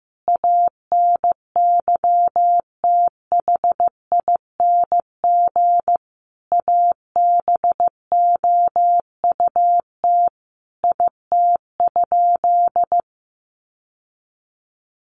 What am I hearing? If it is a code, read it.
Morse code, heard as a keyed tone, about 15 wpm, 700 Hz. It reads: ANYTHING ABOUT IT?
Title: Sentences from Top 1000 Words 15wpm